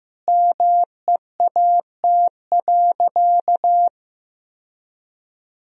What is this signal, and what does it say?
Morse code, heard as a keyed tone, about 15 wpm, 700 Hz. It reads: MEAT.
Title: Sentences from Top 1000 Words - Review Part 3 - 15wpm